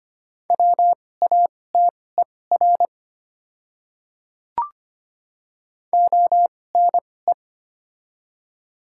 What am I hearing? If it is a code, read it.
Morse code, heard as a keyed tone, about 25 wpm, spaced as for 18 wpm, 700 Hz. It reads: WATER ONE
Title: 2x Character Spacing - Top 100 Words - 25wpm